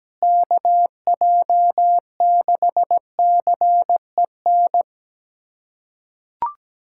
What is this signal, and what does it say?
Morse code, heard as a keyed tone, about 17 wpm, 700 Hz. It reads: KJ6CEN
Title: USA Callsign DE Callsign - 6 Characters - 17wpm